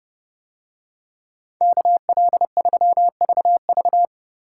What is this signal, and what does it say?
Morse code, heard as a keyed tone, about 30 wpm, 700 Hz. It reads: KL3VV